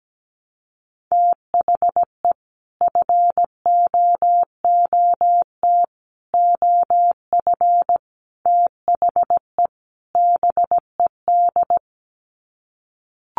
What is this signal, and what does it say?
Morse code, heard as a keyed tone, about 17 wpm, 700 Hz. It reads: THE FOOT OF THE BED
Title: Sets of 5 Words Encore2 17wpm